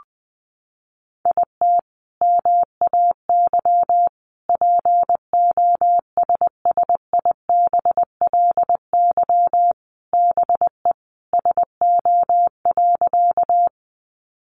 Morse code, heard as a keyed tone, about 20 wpm, 700 Hz. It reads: IT MAY POSSIBLY BE SO.
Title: Sentences from Top 1300 Words 20wpm